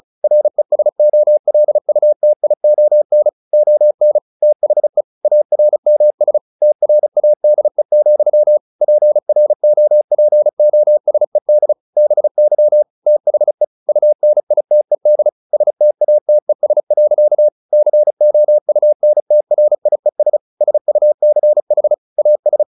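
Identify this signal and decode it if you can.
Morse code, heard as a keyed tone, about 34 wpm, 600 Hz. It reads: RESOLUTION ON THE ARMS TRADE, PROPOSED BY THE UNITED STATES. COUNTRIES SUCH AS
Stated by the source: Morse practice at 35 wpm from CW med Gnister, sending Ken Follett's novel Never in English.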